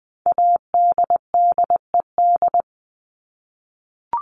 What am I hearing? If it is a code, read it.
Morse code, heard as a keyed tone, about 20 wpm, 700 Hz. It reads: ADDED
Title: ED Suffix - Top 100 Words with Base Words 20wpm